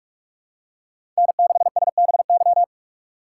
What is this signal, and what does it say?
Morse code, heard as a keyed tone, about 46 wpm, 700 Hz. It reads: N6SBY
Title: USA Call Signs - 5 Characters - 3x Character Spacing - 45wpm